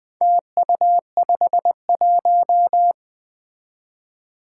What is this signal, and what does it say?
Morse code, heard as a keyed tone, about 21 wpm, 700 Hz. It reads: TU51
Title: Four Challenging Letter-Number Combinations 20wpm